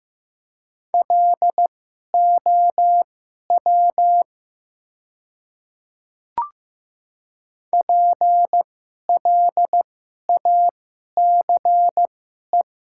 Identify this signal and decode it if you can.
Morse code, heard as a keyed tone, about 15 wpm, spaced as for 11 wpm, 700 Hz. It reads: LOW PLACE
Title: 2x Character Spacing - Top 200 Words - 15wpm